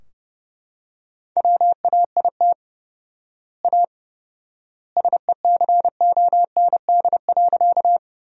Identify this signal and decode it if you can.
Morse code, heard as a keyed tone, about 30 wpm, 700 Hz. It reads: WAIT A SECOND.
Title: 4x Word-Spacing - Sentences from Top 500 Words - 30wpm